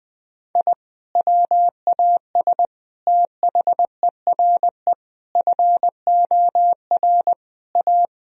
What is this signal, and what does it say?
Morse code, heard as a keyed tone, about 20 wpm, 700 Hz. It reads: I WAS THERE FOR A
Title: Sentences from Top 1000 Words - Review Part 2 - 20wpm